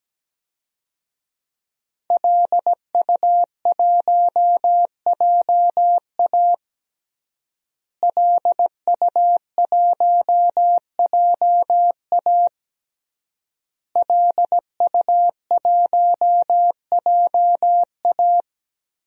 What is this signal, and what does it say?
Morse code, heard as a keyed tone, about 17 wpm, 700 Hz. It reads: LU1JA LU1JA LU1JA
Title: DXCC-100 Call Signs - 5 Characters - Repeated 3x - 17wpm